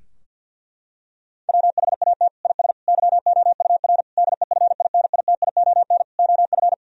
Morse code, heard as a keyed tone, about 50 wpm, 700 Hz. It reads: WHAT IS YOUR DEFINITION OF
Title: Sentences from Top 1000 Words - Review Part 4 - 50wpm